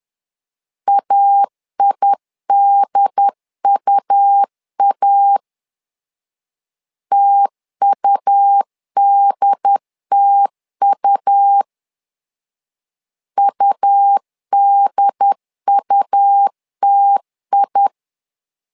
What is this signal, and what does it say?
Morse code, heard as a keyed tone, about 10 wpm, 800 Hz. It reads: AIDUA TUDTU UDUTI
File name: LEZIONI / 2 UHDN / 2 UHD 10wpm 1826